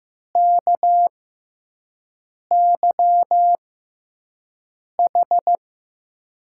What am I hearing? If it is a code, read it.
Morse code, heard as a keyed tone, about 15 wpm, spaced as for 5 wpm, 700 Hz. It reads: KYH
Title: USA Call Signs - 6 Characters - 6x Character Spacing - 15wpm